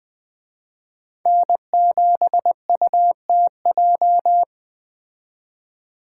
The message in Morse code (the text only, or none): N7UTJ